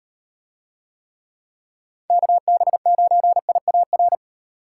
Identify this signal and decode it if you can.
Morse code, heard as a keyed tone, about 38 wpm, 700 Hz. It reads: KB9IAR